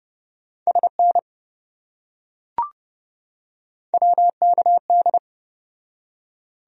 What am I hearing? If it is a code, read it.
Morse code, heard as a keyed tone, about 28 wpm, 700 Hz. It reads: SN WKD